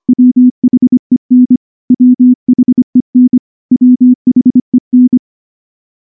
Morse code, heard as a keyed tone, about 23 wpm, 250 Hz. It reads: WHEN WHEN WHEN
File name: morse / morse-training-25WPM